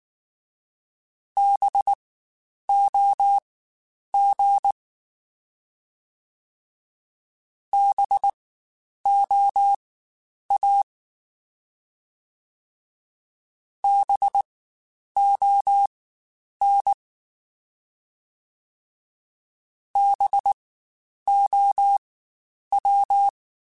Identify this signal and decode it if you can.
Morse code, heard as a keyed tone, about 19 wpm, spaced as for 9 wpm, 800 Hz. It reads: BOG BOA BON BOW
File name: Spaced Alphas / Spaced Alphas mp3s / Bs sp @18